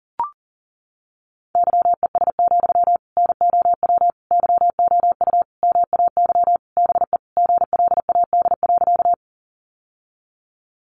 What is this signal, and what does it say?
Morse code, heard as a keyed tone, about 40 wpm, 700 Hz. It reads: YES, NOW YOU MAY BE GLAD.
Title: Sentences from Top 800 Words 40wpm